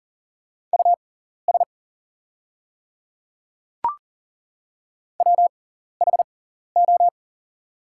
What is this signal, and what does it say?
Morse code, heard as a keyed tone, about 41 wpm, spaced as for 14 wpm, 690 Hz. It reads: US WHO
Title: 6x Character Spacing - Top 300 Words - 40wpm